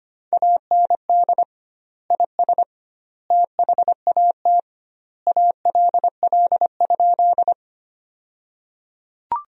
Morse code, heard as a keyed tone, about 25 wpm, 700 Hz. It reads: AND IS THAT ALL?
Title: 2x Word-Spacing - Sentences from Top 100 Words - 25wpm